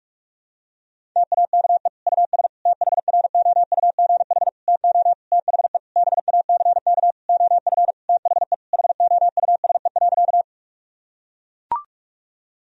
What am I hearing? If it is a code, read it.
Morse code, heard as a keyed tone, about 41 wpm, 700 Hz. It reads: TAKE US THROUGH TO THE BACK OF THE HOUSE.
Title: Sentences from Top 1000 Words - Review Part 1 - 45wpm